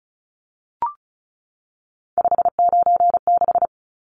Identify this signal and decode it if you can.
Morse code, heard as a keyed tone, about 34 wpm, 700 Hz. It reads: 596
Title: Numbers - 3-Digit 35wpm